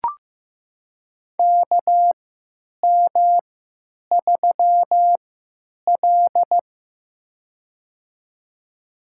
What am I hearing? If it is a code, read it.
Morse code, heard as a keyed tone, about 15 wpm, spaced as for 9 wpm, 700 Hz. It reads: KM3L
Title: USA Call Signs - 4 Characters - 3x Character Spacing - 15wpm